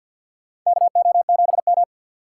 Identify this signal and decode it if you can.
Morse code, heard as a keyed tone, about 52 wpm, 700 Hz. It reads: KO7M
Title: USA Call Signs - 4 Characters - 2x Character Spacing - 50wpm